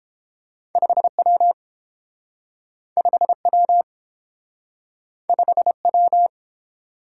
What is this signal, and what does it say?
Morse code, heard as a keyed tone, about 30 wpm, 700 Hz. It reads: 5W 5W 5W